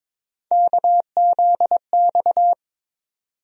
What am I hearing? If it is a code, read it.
Morse code, heard as a keyed tone, about 22 wpm, 700 Hz. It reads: KZX
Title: USA Call Sign Postfixes - 3 Letters 22wpm